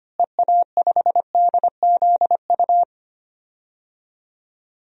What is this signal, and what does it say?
Morse code, heard as a keyed tone, about 25 wpm, 700 Hz. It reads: EA5DZU
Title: DXCC-100 Call Signs - 6 Characters - Repeated 3x - 25wpm